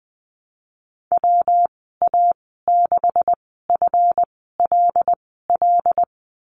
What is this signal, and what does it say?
Morse code, heard as a keyed tone, about 20 wpm, spaced as for 14 wpm, 700 Hz. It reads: WA6FLL
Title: 161 Lesson 26g - Letter g - Call Signs 20wpm